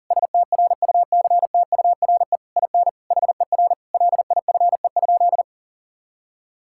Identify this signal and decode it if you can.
Morse code, heard as a keyed tone, about 38 wpm, 700 Hz. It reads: STRUCTURE IN HER LIFE?
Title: Sentences from Top 1000 Words - Review Part 6 - 40wpm